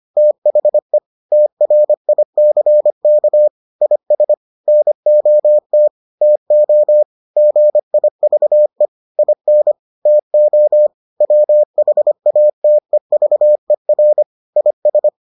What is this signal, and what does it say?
Morse code, heard as a keyed tone, about 25 wpm, 600 Hz. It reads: THE TRICK IS NOT TO GIVE IN TO WHATEVER IS